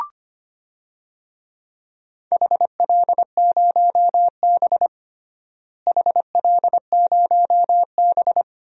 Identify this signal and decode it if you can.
Morse code, heard as a keyed tone, about 25 wpm, 700 Hz. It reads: HL0B HL0B
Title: DXCC-100 Call Signs - 4 Characters - Repeated 2x - 25wpm